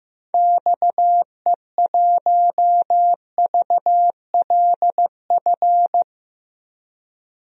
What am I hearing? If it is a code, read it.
Morse code, heard as a keyed tone, about 15 wpm, 700 Hz. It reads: XE1VLF
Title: DXCC-100 Call Signs - 6 Characters - 15wpm